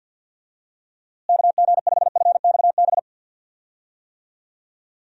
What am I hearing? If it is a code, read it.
Morse code, heard as a keyed tone, about 52 wpm, 700 Hz. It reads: KG5FXB
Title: USA Call Signs - 6 Characters - Repeated 3x - 50wpm